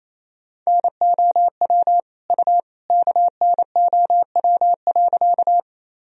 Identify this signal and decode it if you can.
Morse code, heard as a keyed tone, about 28 wpm, 700 Hz. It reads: NOW U KNOW.
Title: Sentences from Top 100 Words and Abbreviations 28wpm